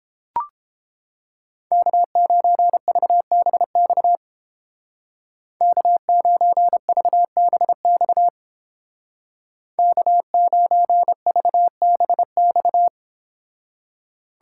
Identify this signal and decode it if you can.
Morse code, heard as a keyed tone, about 30 wpm, 700 Hz. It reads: K9VBX K9VBX K9VBX